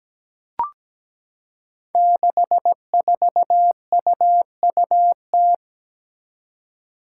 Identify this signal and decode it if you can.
Morse code, heard as a keyed tone, about 17 wpm, 700 Hz. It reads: 64UUT
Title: Five Challenging Letter-Number Combinations 17wpm